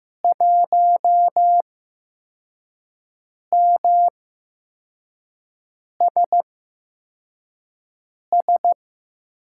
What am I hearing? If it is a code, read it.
Morse code, heard as a keyed tone, about 15 wpm, spaced as for 4 wpm, 700 Hz. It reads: 1MSS
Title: USA Call Signs - 8x Character Spacing - 15wpm